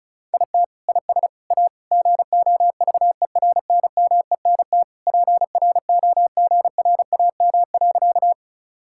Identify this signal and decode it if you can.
Morse code, heard as a keyed tone, about 36 wpm, 700 Hz. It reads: IT IS A GOVERNMENT PROGRAM.
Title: Sentences from Top 1000 Words - Review Part 5 - 35wpm